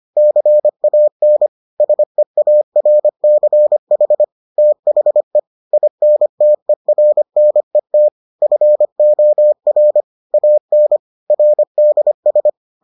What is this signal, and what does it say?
Morse code, heard as a keyed tone, about 25 wpm, 600 Hz. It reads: CAN SEARCH THE INTERNET FOR AN RDS